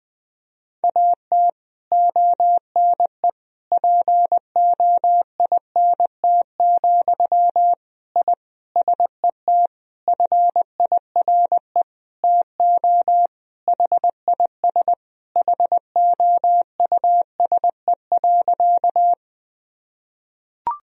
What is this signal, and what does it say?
Morse code, heard as a keyed tone, about 20 wpm, 700 Hz. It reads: AT ONE POINT, I SET FIRE TO HIS HOUSE.
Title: Sentences from Top 1000 Words - Review Part 1 - 20wpm